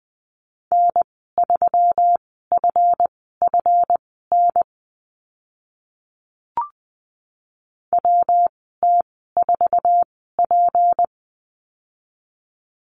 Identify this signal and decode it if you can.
Morse code, heard as a keyed tone, about 20 wpm, spaced as for 14 wpm, 700 Hz. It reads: N3FFN WT4P
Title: 173 Lesson 28g - Number 9 - Call Signs 20wpm